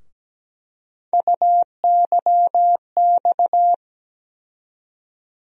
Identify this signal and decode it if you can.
Morse code, heard as a keyed tone, about 17 wpm, 700 Hz. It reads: UYX
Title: USA Call Sign Postfixes - 3 Letters 17wpm